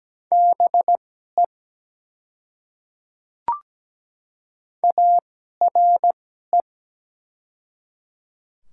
Morse code, heard as a keyed tone, about 17 wpm, spaced as for 12 wpm, 700 Hz. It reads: BE ARE
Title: 2x Character Spacing - Top 100 Words - 17wpm